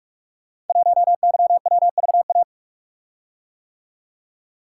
Morse code, heard as a keyed tone, about 44 wpm, 700 Hz. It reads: 1YWVA
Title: Code Groups - 5 Letter:Numbers - 45wpm